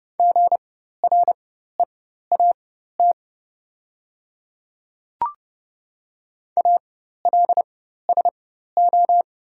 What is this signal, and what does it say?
Morse code, heard as a keyed tone, about 30 wpm, spaced as for 14 wpm, 700 Hz. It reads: GREAT ALSO
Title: 4x Character Spacing - Top 200 Words - 30wpm